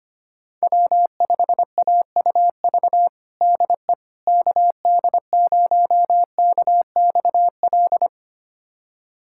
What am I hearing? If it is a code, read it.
Morse code, heard as a keyed tone, about 25 wpm, 700 Hz. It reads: W5AUV DE KD0KXL
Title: USA Callsign DE Callsign 25wpm